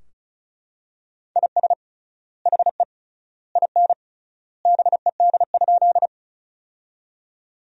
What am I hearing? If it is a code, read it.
Morse code, heard as a keyed tone, about 36 wpm, 700 Hz. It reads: IS HE IN BED?
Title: 3x Word-Spacing - Sentences from Top 500 Words - 35wpm